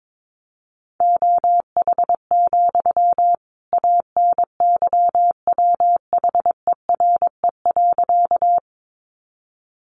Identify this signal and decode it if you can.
Morse code, heard as a keyed tone, about 22 wpm, 700 Hz. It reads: OH, ANYWHERE.
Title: Sentences from Top 1400 Words 22wpm